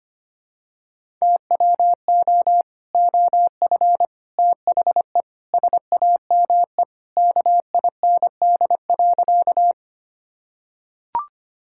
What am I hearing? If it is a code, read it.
Morse code, heard as a keyed tone, about 25 wpm, 700 Hz. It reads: TWO OF THE SAME KIND.
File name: practice / 25wpm / Sentences with 3-Word Phrases - 25wpm